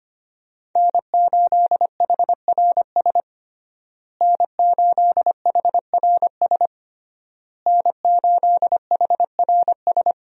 Morse code, heard as a keyed tone, about 25 wpm, 700 Hz. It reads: N8HRS N8HRS N8HRS
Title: USA Call Signs - 5 Characters - Repeated 3x - 25wpm